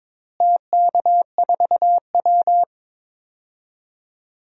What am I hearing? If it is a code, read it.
Morse code, heard as a keyed tone, about 22 wpm, 700 Hz. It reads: TK4W
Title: DXCC-100 Call Signs - 4 Characters - Repeated 3x - 22wpm